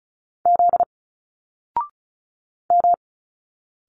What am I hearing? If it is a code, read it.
Morse code, heard as a keyed tone, about 36 wpm, 700 Hz. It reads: Z M
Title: Single Letters 35wpm